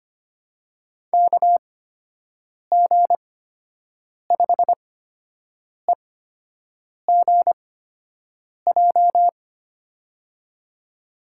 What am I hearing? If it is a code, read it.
Morse code, heard as a keyed tone, about 25 wpm, spaced as for 7 wpm, 700 Hz. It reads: KG5EGJ
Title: USA Call Signs - 8x Character Spacing - 25wpm